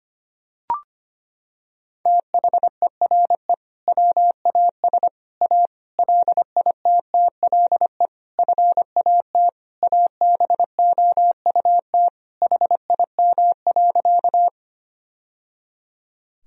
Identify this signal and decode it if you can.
Morse code, heard as a keyed tone, about 25 wpm, 700 Hz. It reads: THERE WAS A LITTLE FAT ABOUT HIM.
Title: Sentences from Top 1000 Words - Review Part 4 - 25wpm